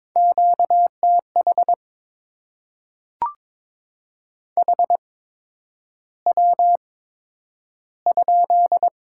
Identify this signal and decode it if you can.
Morse code, heard as a keyed tone, about 22 wpm, spaced as for 9 wpm, 700 Hz. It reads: QTH HW?
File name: Top 20 QSO Elements - 8x Character Spacing - 22wpm